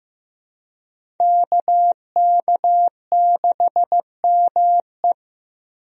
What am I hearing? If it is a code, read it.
Morse code, heard as a keyed tone, about 15 wpm, 700 Hz. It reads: KK6ME
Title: USA Call Signs - 5 Characters - Repeated 2x - 15wpm